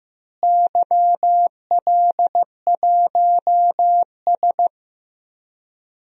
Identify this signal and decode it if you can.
Morse code, heard as a keyed tone, about 15 wpm, 700 Hz. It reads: YL1S
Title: DXCC-100 Call Signs - 4 Characters - Repeated 2x - 15wpm